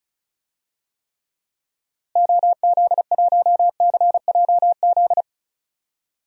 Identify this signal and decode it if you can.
Morse code, heard as a keyed tone, about 36 wpm, 700 Hz. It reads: OZ1CJZ
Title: DXCC-100 Call Signs - 6 Characters - Repeated 3x - 35wpm